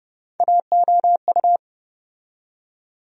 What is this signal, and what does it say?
Morse code, heard as a keyed tone, about 30 wpm, 700 Hz. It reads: AOU
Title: USA Call Sign Postfixes - 3 Letters 30wpm